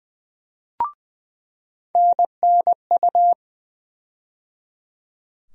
Morse code, heard as a keyed tone, about 20 wpm, 700 Hz. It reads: NNU
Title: USA Call Sign Postfixes - 3 Letters 20wpm